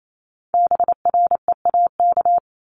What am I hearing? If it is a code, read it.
Morse code, heard as a keyed tone, about 28 wpm, 700 Hz. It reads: BREAK
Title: Multi-letter Phonemes 28wpm